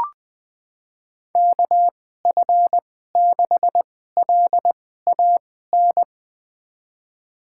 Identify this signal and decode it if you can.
Morse code, heard as a keyed tone, about 20 wpm, spaced as for 15 wpm, 700 Hz. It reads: KF6LAN
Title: USA Call Signs - 6 Characters - 2x Character Spacing - 20wpm